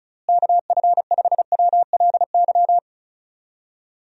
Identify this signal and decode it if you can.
Morse code, heard as a keyed tone, about 34 wpm, 700 Hz. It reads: KF5WLY
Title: USA Call Signs - 6 Characters - Repeated 3x - 35wpm